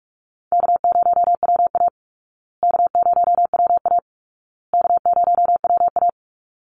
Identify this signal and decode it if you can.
Morse code, heard as a keyed tone, about 44 wpm, 700 Hz. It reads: K0WA K0WA K0WA
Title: Easy Call Signs Repeated 3x - 45wpm